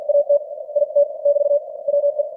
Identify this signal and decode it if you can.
Morse code, heard as a keyed tone, about 50 wpm, spaced as for 44 wpm, 590 Hz. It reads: AT IT <BT> WE
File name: archived / CWBros